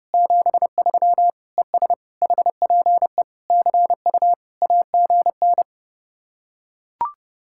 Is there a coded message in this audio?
Yes: Morse code from 0.1 to 5.6 s, heard as a keyed tone, about 30 wpm, 700 Hz. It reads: 73 ES HPE CU AGN